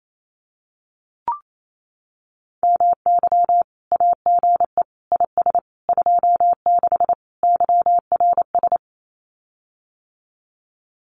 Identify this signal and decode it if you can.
Morse code, heard as a keyed tone, about 28 wpm, 700 Hz. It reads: MY AGE IS 26 YRS
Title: Common QSO Phrases Encore 28wpm